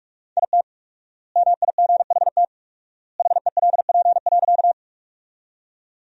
Morse code, heard as a keyed tone, about 47 wpm, 700 Hz. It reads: IT MIGHT HELP.
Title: 4x Word-Spacing - Sentences from Top 300 Words - 45wpm